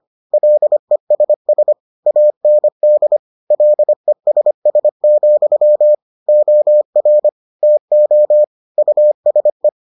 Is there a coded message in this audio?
Yes: Morse code from 0.3 to 9.7 s, heard as a keyed tone, about 25 wpm, 600 Hz. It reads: LESS AND LESS, OR TO USE